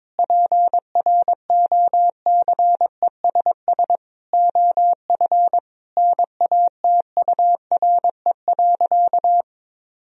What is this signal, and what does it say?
Morse code, heard as a keyed tone, about 22 wpm, 700 Hz. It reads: PROCESS OF NATURE.
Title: Sentences from Top 1000 Words - Review Part 3 - 22wpm